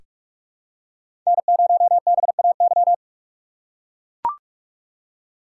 Morse code, heard as a keyed tone, about 43 wpm, 700 Hz. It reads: N0BAY
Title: USA Call Signs - 3x Character Spacing - 45wpm